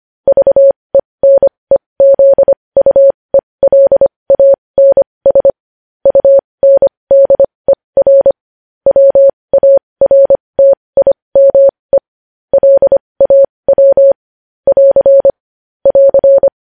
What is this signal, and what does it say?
Morse code, heard as a keyed tone, about 25 wpm, spaced as for 20 wpm, 550 Hz. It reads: VENEZUELANS UNDER WARTIME LAW <AR> <AR>
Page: news-headlines-in-morse-code-at-20-wpm